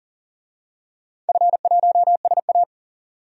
Morse code, heard as a keyed tone, about 40 wpm, 700 Hz. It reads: F1SA
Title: DXCC-100 Call Signs - 4 Characters - 40wpm